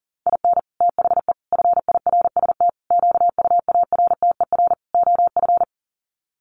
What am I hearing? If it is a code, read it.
Morse code, heard as a keyed tone, about 40 wpm, 700 Hz. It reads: IN THE FIRST QUARTER OF
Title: Sets of 5 Words Encore2 40wpm